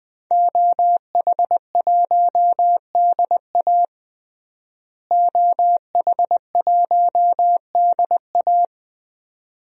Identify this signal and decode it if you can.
Morse code, heard as a keyed tone, about 21 wpm, 700 Hz. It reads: OH1DA OH1DA
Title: DXCC-100 Call Signs - 5 Characters - Repeated 2x - 20wpm